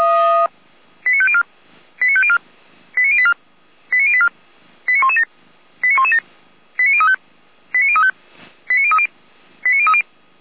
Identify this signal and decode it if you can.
Five-tone selective call (ZVEI), heard as quick runs of five tones, tones of about 70 ms. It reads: 89504 89504 89084 89084 89117 89117 89035 89035 89030 89030